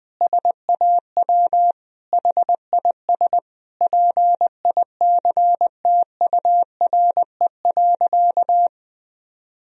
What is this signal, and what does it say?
Morse code, heard as a keyed tone, about 21 wpm, 700 Hz. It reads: SAW HIS PICTURE.
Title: Sentences from Top 1000 Words - Review Part 1 - 20wpm